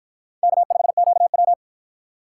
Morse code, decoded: K5QW

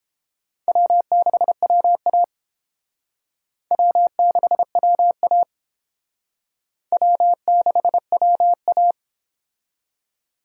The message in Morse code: W6WA W6WA W6WA

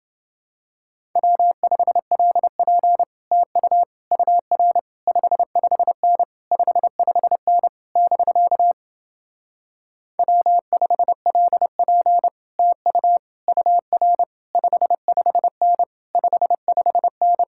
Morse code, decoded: W5LP TU UR 55N 55N <BK> W5LP TU UR 55N 55N

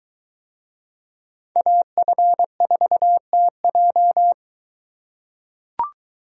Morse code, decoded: AF4TJ

